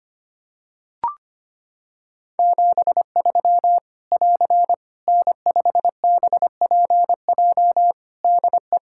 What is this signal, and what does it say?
Morse code, heard as a keyed tone, about 25 wpm, 700 Hz. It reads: 73 <AR> N5BPJ DE
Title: Practice QSO - Exchange 4 - Ending - 25wpm